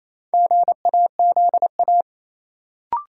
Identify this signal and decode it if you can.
Morse code, heard as a keyed tone, about 28 wpm, 700 Hz. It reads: GAZA
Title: Sets of 5 Words - New Word Drill - 28wpm